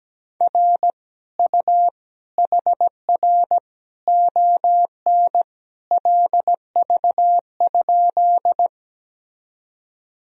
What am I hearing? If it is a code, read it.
Morse code, heard as a keyed tone, about 17 wpm, 700 Hz. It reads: R U HR ON LV?
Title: Sentences from Top 400 Words and Abbreviations 17wpm